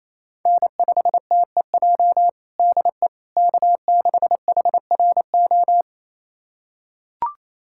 Morse code, heard as a keyed tone, about 28 wpm, 700 Hz. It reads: N5TEJ DE K6HRO
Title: USA Callsign DE Callsign 28wpm